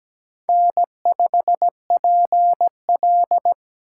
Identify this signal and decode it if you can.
Morse code, heard as a keyed tone, about 17 wpm, 700 Hz. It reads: N5PL
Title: USA Call Signs - 4 Characters - 8x Character Spacing - 17wpm